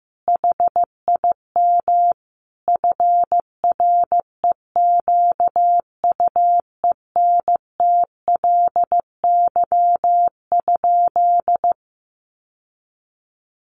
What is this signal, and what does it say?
Morse code, heard as a keyed tone, about 14 wpm, 700 Hz. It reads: HIM FREQUENTLY?
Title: Sentences from Top 1300 Words 15wpm